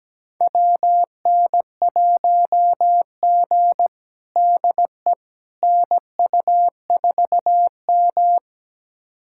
Morse code, heard as a keyed tone, about 17 wpm, 700 Hz. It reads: WN1G DE NU4M
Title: USA Callsign DE Callsign - 4 Characters - 17wpm